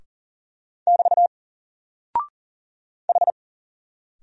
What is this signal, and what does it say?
Morse code, heard as a keyed tone, about 38 wpm, 700 Hz. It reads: <BT> H
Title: Single Character (Letter:Number:Punctuation:Prosign) 40wpm